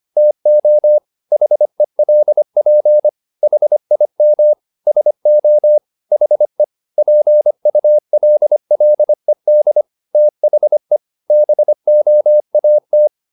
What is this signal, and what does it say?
Morse code, heard as a keyed tone, about 25 wpm, 600 Hz. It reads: TO HELP HIM SO HE PULLED THE BOAT